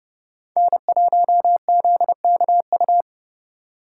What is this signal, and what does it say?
Morse code, heard as a keyed tone, about 39 wpm, 700 Hz. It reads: N1ZKU